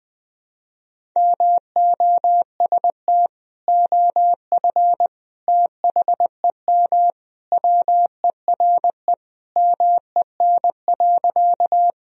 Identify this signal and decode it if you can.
Morse code, heard as a keyed tone, about 21 wpm, 700 Hz. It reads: MOST OF THEM WERE MEN.
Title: Sentences from Top 1000 Words - Review Part 1 - 20wpm